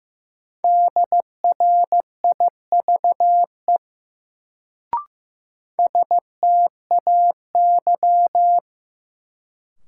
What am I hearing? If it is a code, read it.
Morse code, heard as a keyed tone, about 16 wpm, 700 Hz. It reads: DRIVE STAY
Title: Sentences from Top 500 Words - New Word Drill - 15wpm